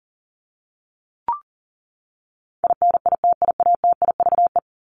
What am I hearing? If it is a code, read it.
Morse code, heard as a keyed tone, about 40 wpm, 700 Hz. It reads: INITIATIVE
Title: Top 1751-2000 Words 40wpm